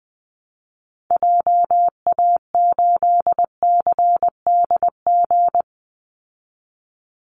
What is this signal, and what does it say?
Morse code, heard as a keyed tone, about 20 wpm, 700 Hz. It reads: JA8CDG
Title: Call Signs Repeated 2x - 20wpm